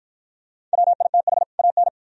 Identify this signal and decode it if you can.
Morse code, heard as a keyed tone, about 47 wpm, 700 Hz. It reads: WITH AN